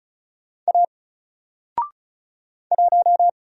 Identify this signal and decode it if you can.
Morse code, heard as a keyed tone, about 36 wpm, 700 Hz. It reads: A 1
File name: Single-Letter-Number-35wpm